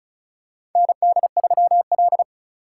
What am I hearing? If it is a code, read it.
Morse code, heard as a keyed tone, about 36 wpm, 700 Hz. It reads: ND3L
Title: K1USN SST QSO Practice - 35wpm